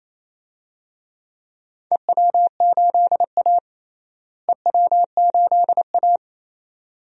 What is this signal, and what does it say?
Morse code, heard as a keyed tone, about 28 wpm, 700 Hz. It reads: EW8A EW8A
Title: DXCC-100 Call Signs - 4 Characters - Repeated 2x - 28wpm